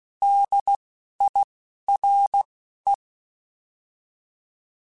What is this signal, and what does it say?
Morse code, heard as a keyed tone, about 16 wpm, spaced as for 12 wpm, 800 Hz. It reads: DIRE
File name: Alphabetical / Alphabetical mp3s / Ds @10